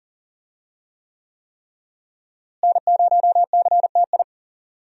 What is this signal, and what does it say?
Morse code, heard as a keyed tone, about 40 wpm, 700 Hz. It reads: N0CTI
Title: USA Call Signs - 5 Characters - Repeated 3x - 40wpm